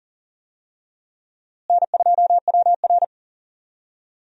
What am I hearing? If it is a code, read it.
Morse code, heard as a keyed tone, about 40 wpm, 700 Hz. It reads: N2WR
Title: USA Call Signs - 4 Characters - Repeated 3x - 40wpm